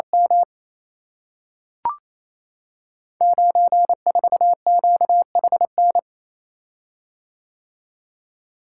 Morse code, heard as a keyed tone, about 28 wpm, 700 Hz. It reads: M 94QHN
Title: Code Groups - 5 Letter:Numbers - 28wpm